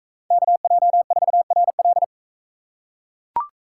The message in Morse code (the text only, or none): KJ4RL